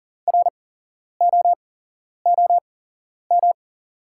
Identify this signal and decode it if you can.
Morse code, heard as a keyed tone, about 41 wpm, spaced as for 11 wpm, 690 Hz. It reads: ROOM
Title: 8x Character Spacing - Top 500 Words - 40wpm